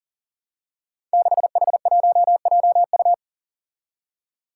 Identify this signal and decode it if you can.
Morse code, heard as a keyed tone, about 42 wpm, 700 Hz. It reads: 6H1JU
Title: Five Challenging Letter-Number Combinations 40wpm